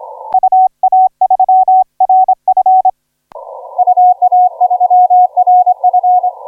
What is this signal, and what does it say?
Morse code, heard as a keyed tone, about 26 wpm, 720 Hz. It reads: UA3RF UA3RF
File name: FS K9EL 0250UTC 17m CW